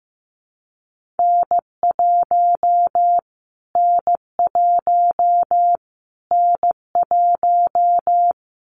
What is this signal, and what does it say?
Morse code, heard as a keyed tone, about 15 wpm, 700 Hz. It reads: N1 N1 N1
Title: Callsign Prefixes for Top 25 DXCC - Repeated 3x - 15wpm